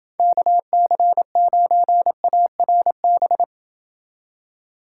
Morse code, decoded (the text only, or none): KC9ARB